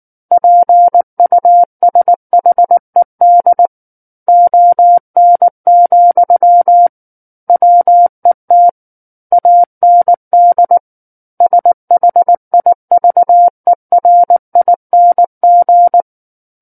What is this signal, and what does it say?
Morse code, heard as a keyed tone, about 19 wpm, 700 Hz. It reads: PUSHED ON, WET AND SHIVERING
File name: Book One / 10-Chapter 10 - 19 WPM